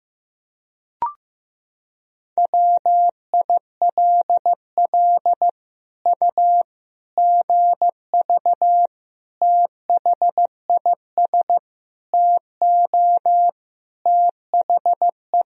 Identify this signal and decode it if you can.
Morse code, heard as a keyed tone, about 15 wpm, 700 Hz. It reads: WILL U GV THIS TO THE
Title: Sentences from Top 200 Words and Abbreviations 15wpm